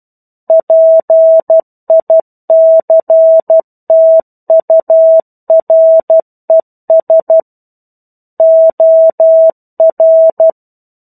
Morse code, heard as a keyed tone, about 12 wpm, 650 Hz. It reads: PICTURES OR
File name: 01-Chapter-01 12 WPM